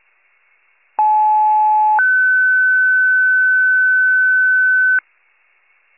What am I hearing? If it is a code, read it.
Two-tone page, heard as a one-second tone, then a three-second tone, about 860 then 1500 Hz. It is heard from 1.0 to 5.0 s.
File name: pager / 94 RESCUE DUTY / wcalert